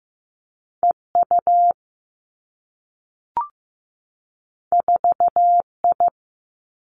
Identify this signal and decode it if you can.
Morse code, heard as a keyed tone, about 15 wpm, 700 Hz. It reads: E U 4 I
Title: Two Challenging Letter-Number Combinations 15wpm